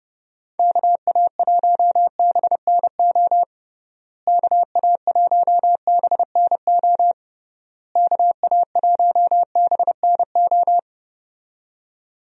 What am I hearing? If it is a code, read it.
Morse code, heard as a keyed tone, about 29 wpm, 700 Hz. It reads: KA1BNO KA1BNO KA1BNO
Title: USA Call Signs - 6 Characters - Repeated 3x - 30wpm